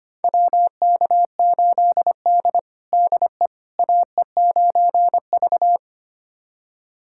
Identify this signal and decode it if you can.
Morse code, heard as a keyed tone, about 25 wpm, 700 Hz. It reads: WK8D DE AE9V
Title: USA Callsign DE Callsign - 4 Characters - 25wpm